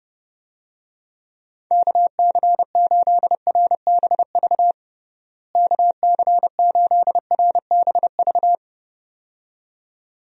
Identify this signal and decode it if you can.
Morse code, heard as a keyed tone, about 32 wpm, 700 Hz. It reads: KC8RBV KC8RBV
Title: USA Call Signs - 6 Characters - Repeated 2x - 30wpm